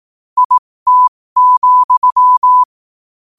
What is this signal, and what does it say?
Morse code, heard as a keyed tone, about 18 wpm, spaced as for 15 wpm, 1000 Hz. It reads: IT,
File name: qotdfiles / 15wpm